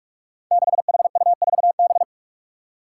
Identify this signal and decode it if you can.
Morse code, heard as a keyed tone, about 46 wpm, 700 Hz. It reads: 6HU4B
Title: Five Challenging Letter-Number Combinations 45wpm